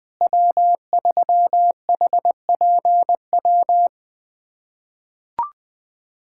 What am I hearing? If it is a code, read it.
Morse code, heard as a keyed tone, about 20 wpm, 700 Hz. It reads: W3HPW
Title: USA Callsign DE Callsign - 5 Characters - 20wpm